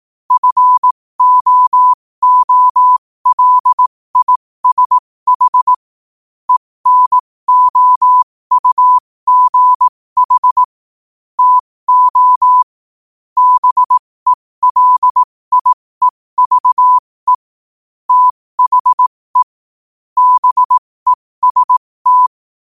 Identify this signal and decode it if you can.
Morse code, heard as a keyed tone, about 18 wpm, spaced as for 15 wpm, 1000 Hz. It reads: FOOLISH ENOUGH TO BELIEVE THE BEST
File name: qotdfiles / 15wpm